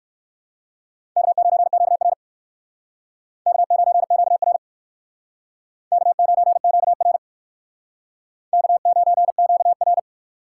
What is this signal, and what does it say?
Morse code, heard as a keyed tone, about 59 wpm, 700 Hz. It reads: K9QR K9QR K9QR K9QR